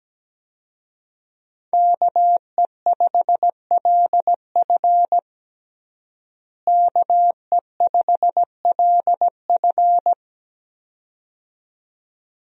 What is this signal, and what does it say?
Morse code, heard as a keyed tone, about 17 wpm, 700 Hz. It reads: KE5LF KE5LF
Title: USA Call Signs - 5 Characters - Repeated 2x - 17wpm